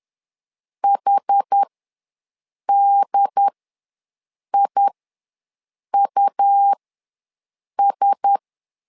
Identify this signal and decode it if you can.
Morse code, heard as a keyed tone, about 10 wpm, spaced as for 6 wpm, 800 Hz. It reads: HDIUS
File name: LEZIONI / 4 Q1S ws 10wpm 2208